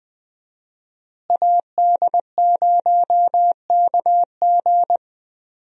Morse code, heard as a keyed tone, about 20 wpm, 700 Hz. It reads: AD0KG
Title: USA Call Signs - 5 Characters - Repeated 3x - 20wpm